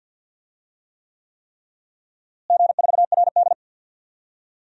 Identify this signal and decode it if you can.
Morse code, heard as a keyed tone, about 50 wpm, 700 Hz. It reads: G4RD
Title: DXCC-100 Call Signs - 4 Characters - Repeated 3x - 50wpm